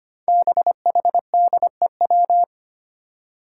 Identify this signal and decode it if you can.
Morse code, heard as a keyed tone, about 25 wpm, 700 Hz. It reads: BHDEW
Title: Five Challenging Letter-Number Combinations 25wpm